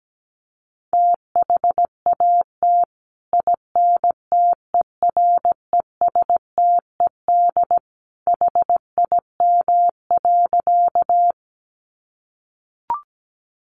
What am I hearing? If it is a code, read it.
Morse code, heard as a keyed tone, about 17 wpm, 700 Hz. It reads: THAT INTERESTED HIM.
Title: Sentences from Top 900 Words 17wpm